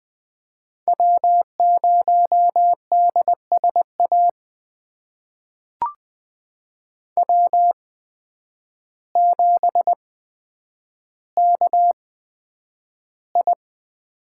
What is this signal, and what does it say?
Morse code, heard as a keyed tone, about 21 wpm, spaced as for 9 wpm, 700 Hz. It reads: W0DSA W7KI
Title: USA Call Signs - 5 Characters - 8x Character Spacing - 20wpm